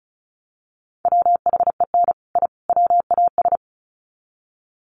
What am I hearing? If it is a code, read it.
Morse code, heard as a keyed tone, about 34 wpm, 700 Hz. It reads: WHEN I WAS
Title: Sets of 3 Words 35wpm